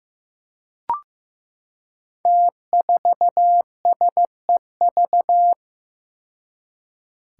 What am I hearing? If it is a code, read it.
Morse code, heard as a keyed tone, about 15 wpm, 700 Hz. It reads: T4SEV
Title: Five Challenging Letter-Number Combinations 15wpm